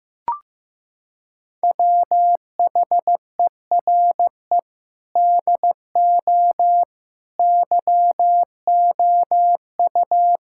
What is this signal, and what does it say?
Morse code, heard as a keyed tone, about 15 wpm, 700 Hz. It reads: WHERE DO YOU
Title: Sentences from Top 1000 Words - Review Part 5 - 15wpm